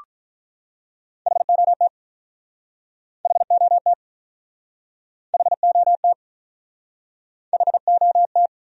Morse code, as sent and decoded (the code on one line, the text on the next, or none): .... --- - / .... --- - / .... --- - / .... --- -
HOT HOT HOT HOT